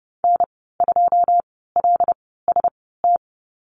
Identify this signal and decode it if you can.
Morse code, heard as a keyed tone, about 30 wpm, spaced as for 17 wpm, 700 Hz. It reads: N2LST
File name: Lesson 14g - Number 2 - Call Signs 30wpm